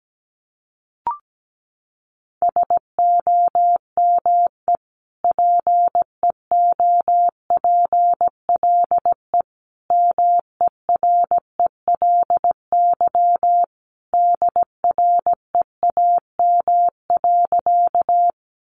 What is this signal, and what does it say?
Morse code, heard as a keyed tone, about 17 wpm, 700 Hz. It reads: SOME PEOPLE MERELY DREAM.
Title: Sentences from Top 1500 Words 17wpm